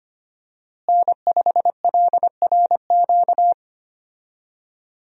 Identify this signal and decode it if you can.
Morse code, heard as a keyed tone, about 25 wpm, 700 Hz. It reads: N5LRQ
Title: USA Call Signs - 5 Characters - Repeated 2x - 25wpm